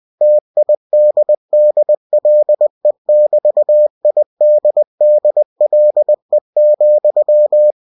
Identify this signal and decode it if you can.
Morse code, heard as a keyed tone, about 20 wpm, 600 Hz. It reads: TIDDLE<BT>IDDLE,
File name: cw-books / winnie-the-pooh / winnie-20wpm-page-0013